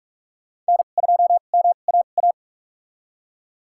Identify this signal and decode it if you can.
Morse code, heard as a keyed tone, about 46 wpm, spaced as for 33 wpm, 700 Hz. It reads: N2MAA
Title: USA Call Signs - 5 Characters - 2x Character Spacing - 45wpm